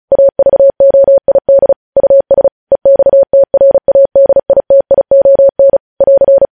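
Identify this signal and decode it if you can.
Morse code, heard as a keyed tone, about 36 wpm, 550 Hz. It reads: AVOID US EXTRADITION <AR>
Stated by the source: Morse code news headlines sent at 35.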